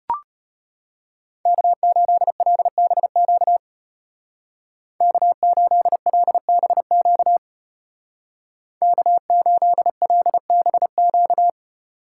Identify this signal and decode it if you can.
Morse code, heard as a keyed tone, about 32 wpm, 700 Hz. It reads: K8LBQ K8LBQ K8LBQ